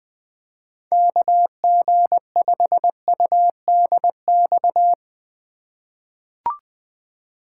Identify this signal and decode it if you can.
Morse code, heard as a keyed tone, about 20 wpm, 700 Hz. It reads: KG5UDX
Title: USA Call Signs - 6 Characters - 4x Character Spacing - 20wpm